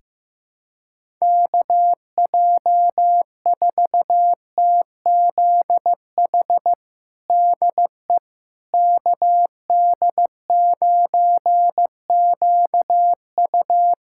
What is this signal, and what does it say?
Morse code, heard as a keyed tone, about 16 wpm, 700 Hz. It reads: KJ4TZH DE KD9QU
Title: USA Callsign DE Callsign 15wpm